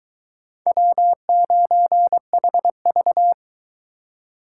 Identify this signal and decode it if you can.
Morse code, heard as a keyed tone, about 22 wpm, 700 Hz. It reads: W9HV